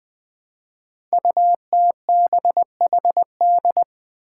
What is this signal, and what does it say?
Morse code, heard as a keyed tone, about 20 wpm, 700 Hz. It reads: UTBHD
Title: Five Challenging Letter-Number Combinations 20wpm